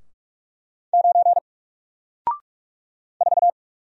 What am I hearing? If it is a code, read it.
Morse code, heard as a keyed tone, about 44 wpm, 700 Hz. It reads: 9 4
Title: Single Character (Letter:Number:Punctuation:Prosign) 45wpm